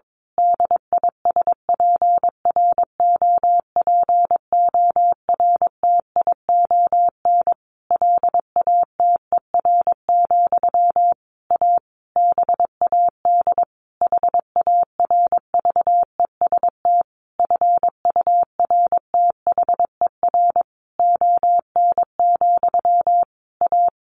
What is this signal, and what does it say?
Morse code, heard as a keyed tone, about 22 wpm, 700 Hz. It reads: DISPROPORTION LATER, A BAD HARVEST FURTHER ON, A
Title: A Sound Of Thunder 22wpm